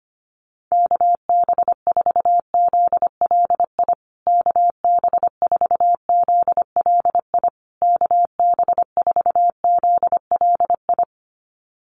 Morse code, heard as a keyed tone, about 25 wpm, 700 Hz. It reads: KB4ZLI KB4ZLI KB4ZLI
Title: USA Call Signs Repeated 3x 25wpm